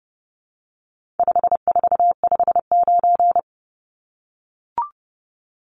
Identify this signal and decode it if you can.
Morse code, heard as a keyed tone, about 30 wpm, 700 Hz. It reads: 5459
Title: Numbers - 4-Digit 30wpm